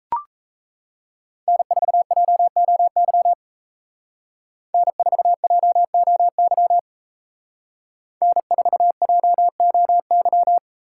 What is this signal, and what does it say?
Morse code, heard as a keyed tone, about 33 wpm, 700 Hz. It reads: N4JOY N4JOY N4JOY
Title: Speed Racing - USA 1x3 Call Signs 28wpm